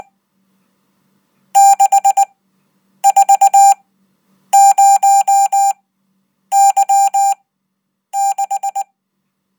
Morse code, heard as a keyed tone, about 19 wpm, spaced as for 8 wpm, 790 Hz. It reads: 640Y6
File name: Code